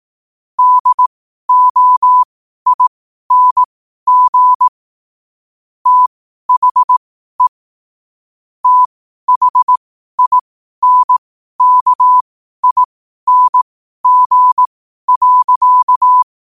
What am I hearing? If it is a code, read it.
Morse code, heard as a keyed tone, about 18 wpm, spaced as for 12 wpm, 1000 Hz. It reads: DOING THE THINKING.
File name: qotdfiles / 12wpm